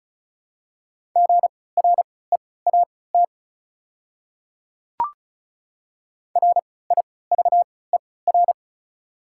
Morse code, heard as a keyed tone, about 34 wpm, spaced as for 20 wpm, 700 Hz. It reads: GREAT RIVER
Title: 3x Character Spacing - Top 300 Words - 35wpm